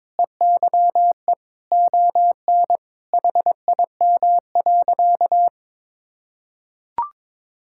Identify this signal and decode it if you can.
Morse code, heard as a keyed tone, about 22 wpm, 700 Hz. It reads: EYE ON HIM.
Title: Sentences from Top 1600 Words 22wpm